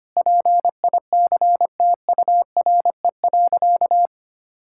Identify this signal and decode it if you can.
Morse code, heard as a keyed tone, about 25 wpm, 700 Hz. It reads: PICTURE.
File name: practice / 25wpm / Sentences with 4-Word Phrases - 25wpm